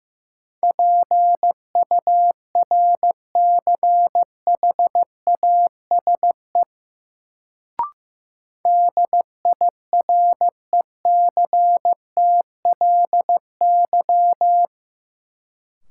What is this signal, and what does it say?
Morse code, heard as a keyed tone, about 15 wpm, 700 Hz. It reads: PURCHASE DIRECTLY
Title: Sentences from Top 800 Words - New Word Drill - 15wpm